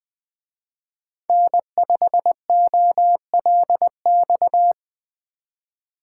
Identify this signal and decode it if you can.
Morse code, heard as a keyed tone, about 23 wpm, 700 Hz. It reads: N5OLX
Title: Speed Racing - USA 1x3 Call Signs 20wpm